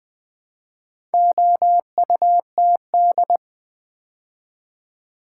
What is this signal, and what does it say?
Morse code, heard as a keyed tone, about 20 wpm, 700 Hz. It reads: OUTD
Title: Build A Word - 2 Letters At A Time - 8-Letter Words - 20wpm